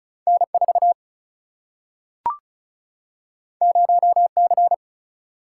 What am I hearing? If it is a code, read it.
Morse code, heard as a keyed tone, about 36 wpm, 700 Hz. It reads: N4 0C
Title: Single-Letter Single-Number 35wpm